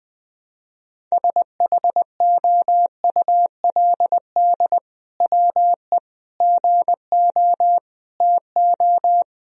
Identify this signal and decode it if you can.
Morse code, heard as a keyed tone, about 21 wpm, 700 Hz. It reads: SHOULD WE GO TO
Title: Sentences from Top 1000 Words - Review Part 3 - 20wpm